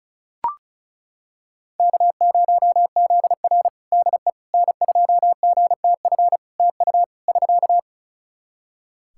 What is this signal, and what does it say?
Morse code, heard as a keyed tone, about 36 wpm, 700 Hz. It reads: K0ZR DE N2GTF TU <SK>
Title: Practice QSO - Exchange 4 - Ending - 35wpm